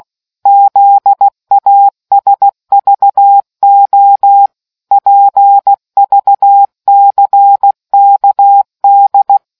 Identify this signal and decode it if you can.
Morse code, heard as a keyed tone, about 16 wpm, 800 Hz. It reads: ZASVO PVCKD